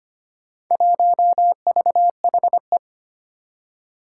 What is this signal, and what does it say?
Morse code, heard as a keyed tone, about 25 wpm, 700 Hz. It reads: 1VHE
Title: Four Challenging Letter-Number Combinations 25wpm